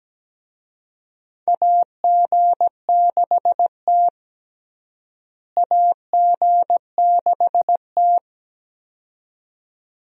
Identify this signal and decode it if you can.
Morse code, heard as a keyed tone, about 17 wpm, 700 Hz. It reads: AG6T AG6T
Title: USA Call Signs - 4 Characters - Repeated 2x - 17wpm